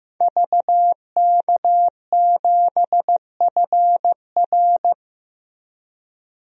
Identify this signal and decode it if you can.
Morse code, heard as a keyed tone, about 15 wpm, 700 Hz. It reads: VK7FR
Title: DXCC-100 Call Signs - 5 Characters - 15wpm